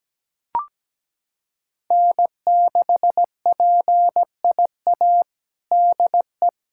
Morse code, heard as a keyed tone, about 17 wpm, 700 Hz. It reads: N6PIA DE
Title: USA Callsign DE Callsign - 5 Characters - 17wpm